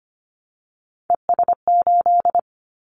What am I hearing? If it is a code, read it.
Morse code, heard as a keyed tone, about 25 wpm, 700 Hz. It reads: ES8
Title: Callsign Prefixes for Top 100 DXCC - 25wpm